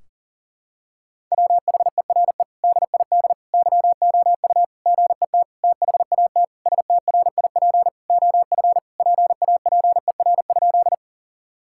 Morse code, heard as a keyed tone, about 40 wpm, 700 Hz. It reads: WHERE DID YOU GET THAT STRIP OF PAPER?
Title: Sentences from Top 1600 Words 40wpm